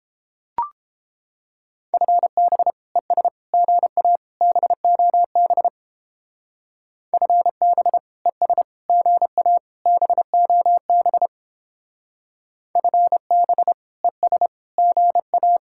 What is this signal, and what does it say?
Morse code, heard as a keyed tone, about 29 wpm, 700 Hz. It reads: FB ES GA BOB FB ES GA BOB FB ES GA